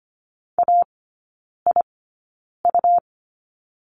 Morse code, heard as a keyed tone, about 25 wpm, spaced as for 9 wpm, 700 Hz. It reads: AIU
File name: course / 25:10wpm / 100 Lesson 17e - Letter u - Three Characters 25wpm